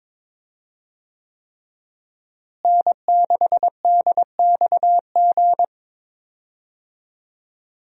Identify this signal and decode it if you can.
Morse code, heard as a keyed tone, about 22 wpm, 700 Hz. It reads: N6DXG
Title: USA Call Signs - 5 Characters - Repeated 3x - 22wpm